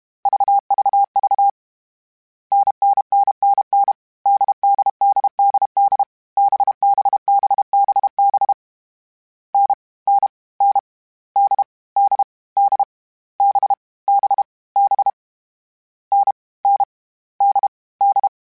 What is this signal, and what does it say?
Morse code, heard as a keyed tone, about 32 wpm, 800 Hz. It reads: VVV NNNNN DDDDD BBBBB N N N D D D B B B N N D D
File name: Difficult Combos / Difficult Combos-mp3s / NDB NDB6 @30